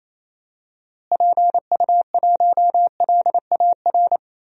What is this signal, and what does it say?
Morse code, heard as a keyed tone, about 28 wpm, 700 Hz. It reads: PU1LAR
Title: DXCC-100 Call Signs - 6 Characters - Repeated 3x - 28wpm